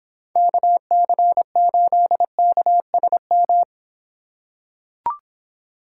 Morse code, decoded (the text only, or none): KC8KSM